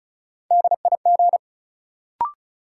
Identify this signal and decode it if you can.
Morse code, heard as a keyed tone, about 36 wpm, 700 Hz. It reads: DIG
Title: Sentences from Top 1600 Words - New Word Drill Alt - 35wpm